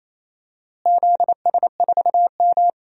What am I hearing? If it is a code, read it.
Morse code, heard as a keyed tone, about 28 wpm, 700 Hz. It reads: ZS4M
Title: DXCC-100 Call Signs - 4 Characters - Repeated 2x - 28wpm